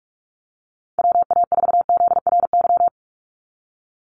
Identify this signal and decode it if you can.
Morse code, heard as a keyed tone, about 44 wpm, 700 Hz. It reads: WA4ZRY
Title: USA Call Signs 45wpm